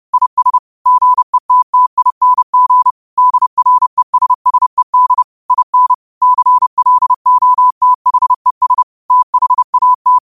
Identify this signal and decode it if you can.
Morse code, heard as a keyed tone, about 31 wpm, 1000 Hz. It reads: IS GETTING DRESSED IN CLOTHES THAT
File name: qotdfiles / 30wpm